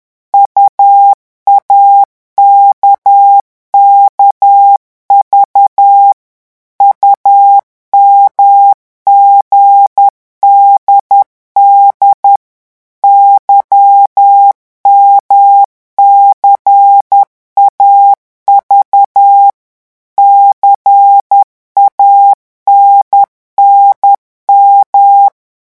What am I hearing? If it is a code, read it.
Morse code, heard as a keyed tone, about 10 wpm, 800 Hz. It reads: UAKKV UMGDD YMCAV CANNM
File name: morse mp3 / revision3-vitesse 10 mots